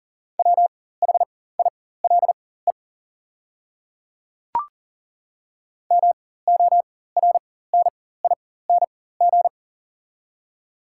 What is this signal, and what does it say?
Morse code, heard as a keyed tone, about 40 wpm, spaced as for 19 wpm, 690 Hz. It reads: WHILE MORNING
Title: 4x Character Spacing - Top 500 Words - 40wpm